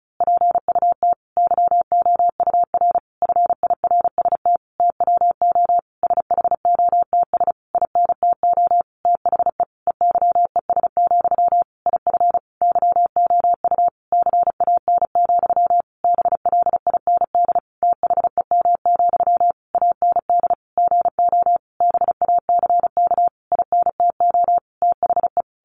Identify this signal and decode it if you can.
Morse code, heard as a keyed tone, about 34 wpm, 700 Hz. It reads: PUT YOUR FIRST TWO SHOTS INTO THE EYES, IF YOU CAN, BLIND THEM, AND GO BACK INTO THE
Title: A Sound Of Thunder 35wpm